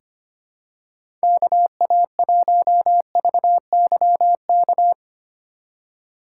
Morse code, heard as a keyed tone, about 25 wpm, 700 Hz. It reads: KA1VYK
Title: USA Call Signs - 6 Characters - 2x Character Spacing - 25wpm